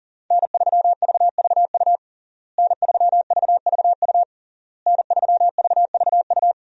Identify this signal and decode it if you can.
Morse code, heard as a keyed tone, about 40 wpm, 700 Hz. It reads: N3VVU N3VVU N3VVU
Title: USA Call Signs - 5 Characters - Repeated 3x - 40wpm